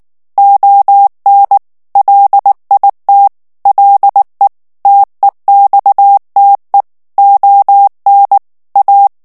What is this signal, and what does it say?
Morse code, heard as a keyed tone, about 19 wpm, 800 Hz. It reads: ON LIT LE TEXTE ON A
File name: morse mp3 / FRANCAIS TEXTE CLAIR 1 VITESSE 18 MOTS MINUTES